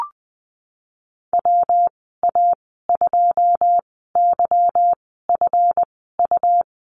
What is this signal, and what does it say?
Morse code, heard as a keyed tone, about 20 wpm, spaced as for 15 wpm, 700 Hz. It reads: WA2YFU
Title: 154 Lesson 25g - Letter p - Call Signs 20wpm